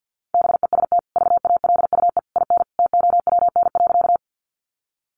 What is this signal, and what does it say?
Morse code, heard as a keyed tone, about 50 wpm, 700 Hz. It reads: BEST VALUE IN TOWN.
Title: Sentences from Top 600 Words 50wpm